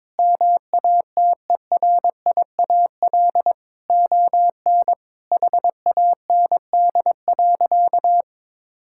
Morse code, heard as a keyed tone, about 22 wpm, 700 Hz. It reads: MATERIAL ON HAND.
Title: Sentences from Top 1000 Words - Review Part 5 - 22wpm